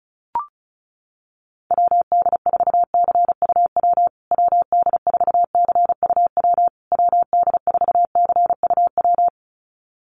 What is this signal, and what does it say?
Morse code, heard as a keyed tone, about 34 wpm, 700 Hz. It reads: WD4CUW WD4CUW WD4CUW
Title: USA Call Signs Repeated 3x 35wpm